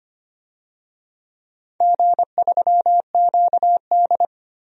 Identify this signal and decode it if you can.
Morse code, heard as a keyed tone, about 25 wpm, 700 Hz. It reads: G3QD
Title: DXCC-100 Call Signs - 4 Characters - 25wpm